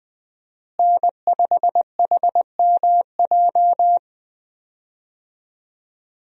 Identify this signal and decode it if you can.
Morse code, heard as a keyed tone, about 20 wpm, 700 Hz. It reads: N5HMJ
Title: USA Call Signs - 5 Characters - Repeated 3x - 20wpm